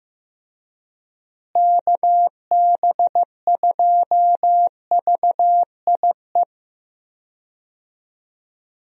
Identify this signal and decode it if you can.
Morse code, heard as a keyed tone, about 15 wpm, 700 Hz. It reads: KB2VIE
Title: USA Call Signs - 6 Characters - Repeated 2x - 15wpm